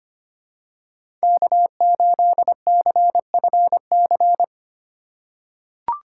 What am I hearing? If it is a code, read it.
Morse code, heard as a keyed tone, about 25 wpm, 700 Hz. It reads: K8CFC
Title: USA Call Signs - 5 Characters - 3x Character Spacing - 25wpm